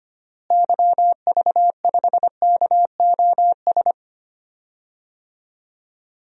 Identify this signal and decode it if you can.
Morse code, heard as a keyed tone, about 25 wpm, 700 Hz. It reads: YV5KOS
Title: DXCC-100 Call Signs - 6 Characters - Repeated 2x - 25wpm